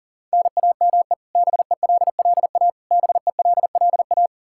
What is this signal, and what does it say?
Morse code, heard as a keyed tone, about 40 wpm, 700 Hz. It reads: NAME BELLA BELLA
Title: Practice QSO - Exchange 1 - Beginning - 40wpm